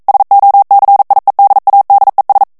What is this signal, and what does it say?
Morse code, heard as a keyed tone, about 42 wpm, 800 Hz. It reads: SOCIEDADES
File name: morse mp3 / ESPAGNOL TEXTE EN CLAIR 1 VITESSE 40 MOTS MINUTES